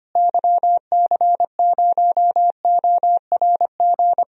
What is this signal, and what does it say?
Morse code, heard as a keyed tone, about 25 wpm, 700 Hz. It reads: YC0ORG
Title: DXCC-100 Call Signs - 6 Characters - Repeated 2x - 25wpm